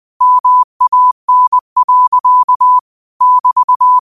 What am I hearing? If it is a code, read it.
Morse code, heard as a keyed tone, about 20 wpm, 1000 Hz. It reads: MAN. <BT>